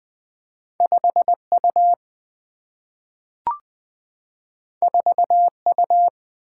Morse code, heard as a keyed tone, about 20 wpm, 700 Hz. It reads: 5U 4U
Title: Single-Letter Single-Number 20wpm